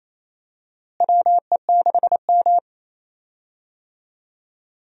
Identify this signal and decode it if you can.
Morse code, heard as a keyed tone, about 27 wpm, 700 Hz. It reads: WE6M
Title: USA Call Signs - 4 Characters - Repeated 3x - 28wpm